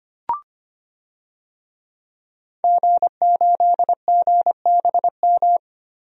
Morse code, heard as a keyed tone, about 25 wpm, 700 Hz. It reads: G8GBM
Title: DXCC-100 Call Signs - 5 Characters - Repeated 2x - 25wpm